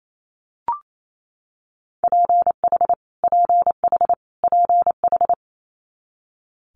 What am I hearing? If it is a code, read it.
Morse code, heard as a keyed tone, about 28 wpm, 700 Hz. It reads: PH PH PH
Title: Multi-letter Phonemes Repeated 3x 28wpm